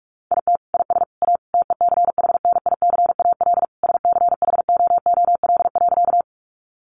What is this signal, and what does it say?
Morse code, heard as a keyed tone, about 46 wpm, 700 Hz. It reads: IT IS A TECHNICAL SCHOOL.
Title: Sentences from Top 1100 Words 45wpm